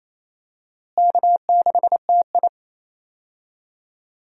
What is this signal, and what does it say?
Morse code, heard as a keyed tone, about 27 wpm, 700 Hz. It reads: K6TI
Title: USA Call Signs - 4 Characters - Repeated 2x - 28wpm